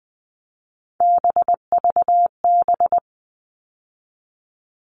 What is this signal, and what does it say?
Morse code, heard as a keyed tone, about 20 wpm, 700 Hz. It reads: BVB
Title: All Cut Numbers - 3-Digit 20wpm